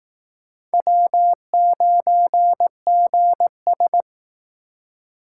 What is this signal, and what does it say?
Morse code, heard as a keyed tone, about 18 wpm, 700 Hz. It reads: W9GS
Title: Speed Racing - USA 1x2 Call Signs 15wpm